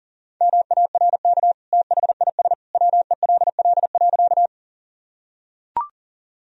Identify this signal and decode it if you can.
Morse code, heard as a keyed tone, about 40 wpm, 700 Hz. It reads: MARK THIS WELL.
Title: Sentences from Top 1000 Words - Review Part 1 - 40wpm